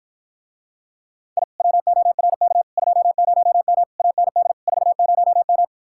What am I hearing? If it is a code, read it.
Morse code, heard as a keyed tone, about 54 wpm, 700 Hz. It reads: I WORK 20M AND 40M